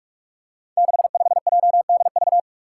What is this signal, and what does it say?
Morse code, heard as a keyed tone, about 46 wpm, 700 Hz. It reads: 65JDV
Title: Five Challenging Letter-Number Combinations 45wpm